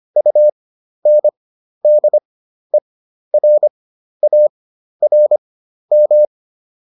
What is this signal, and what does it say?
Morse code, heard as a keyed tone, about 25 wpm, spaced as for 12 wpm, 600 Hz. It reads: UNDERARM